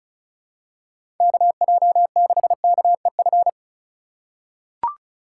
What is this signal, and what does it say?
Morse code, heard as a keyed tone, about 36 wpm, 700 Hz. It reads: KJ6KEF
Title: USA Call Signs - 6 Characters - 3x Character Spacing - 35wpm